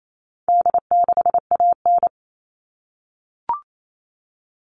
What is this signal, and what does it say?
Morse code, heard as a keyed tone, about 28 wpm, 700 Hz. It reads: D6AN
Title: All Cut Numbers - 4-Digit 28wpm